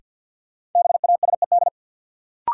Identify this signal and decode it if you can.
Morse code, heard as a keyed tone, about 52 wpm, 700 Hz. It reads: BASED
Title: Sets of 2 Words - New Word Drill - 50wpm